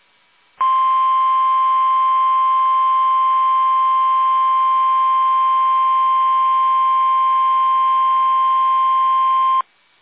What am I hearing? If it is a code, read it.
Emergency Alert System (SAME) tones (the 1050 Hz attention tone). It is heard from 0.6 to 9.6 s.
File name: eas / KUSO / KUSO